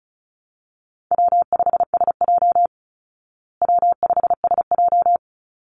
Morse code, heard as a keyed tone, about 34 wpm, 700 Hz. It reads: W5SJ W5SJ
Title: Call Signs Repeated 3x - 35wpm